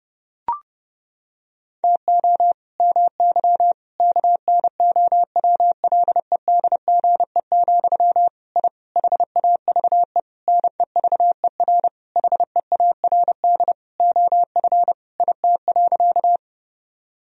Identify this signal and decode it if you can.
Morse code, heard as a keyed tone, about 32 wpm, 700 Hz. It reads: TO MY KNOWLEDGE, I HAVE NEVER HEARD OF IT.
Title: Sentences from Top 1000 Words - Review Part 3 - 30wpm